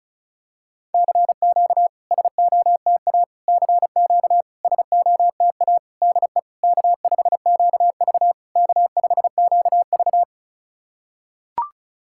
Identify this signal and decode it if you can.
Morse code, heard as a keyed tone, about 34 wpm, 690 Hz. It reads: CQ SOTA CQ SOTA DE K5QV K5QV
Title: SOTA QSO Practice 35wpm